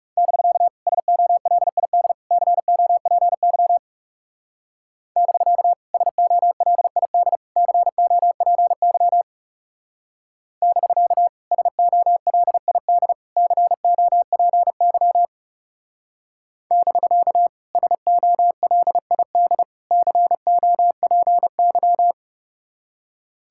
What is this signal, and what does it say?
Morse code, heard as a keyed tone, about 36 wpm, 700 Hz. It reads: <BK> SOLID COPY <BK> SOLID COPY <BK> SOLID COPY <BK> SOLID COPY
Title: Speed-Racing Common QSO Phrases - 30wpm